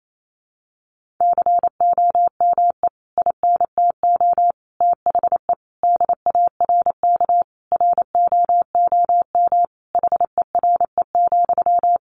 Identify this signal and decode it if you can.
Morse code, heard as a keyed tone, about 28 wpm, 700 Hz. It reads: COME INTO THE DARK ROOM HERE,
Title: Sentences from Top 1000 Words 28wpm